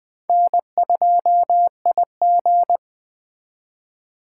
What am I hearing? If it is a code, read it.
Morse code, heard as a keyed tone, about 23 wpm, 700 Hz. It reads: N2IG
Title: Speed Racing - USA 1x2 Call Signs 20wpm